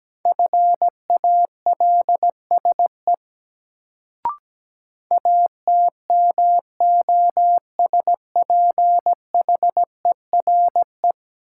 Morse code, heard as a keyed tone, about 17 wpm, 700 Hz. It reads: FALSE ATMOSPHERE
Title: Sentences from Top 1600 Words - New Word Drill Alt - 17wpm